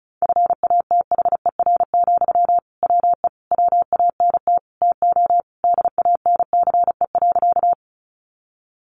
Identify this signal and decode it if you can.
Morse code, heard as a keyed tone, about 34 wpm, 700 Hz. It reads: FATHER, WE WANT TO DANCE.
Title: Sentences from Top 700 Words 35wpm